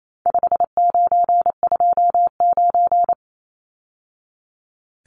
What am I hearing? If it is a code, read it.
Morse code, heard as a keyed tone, about 28 wpm, 700 Hz. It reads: 5929
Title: Numbers - 4-Digit 28wpm